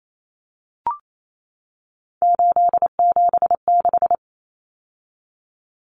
Morse code, heard as a keyed tone, about 28 wpm, 700 Hz. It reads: 876
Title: Numbers - 3-Digit 28wpm